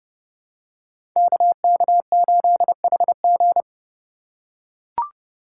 Morse code, heard as a keyed tone, about 30 wpm, 700 Hz. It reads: KK8HG
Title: Code Groups - 5 Letter:Numbers - 30wpm